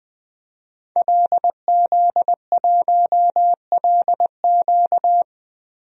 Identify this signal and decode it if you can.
Morse code, heard as a keyed tone, about 20 wpm, 700 Hz. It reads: LZ1LQ
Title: DXCC-100 Call Signs - 5 Characters - 20wpm